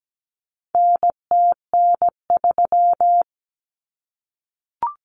Morse code, heard as a keyed tone, about 17 wpm, 700 Hz. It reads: NTN3
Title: Common Cut Numbers - 4-Digit 17wpm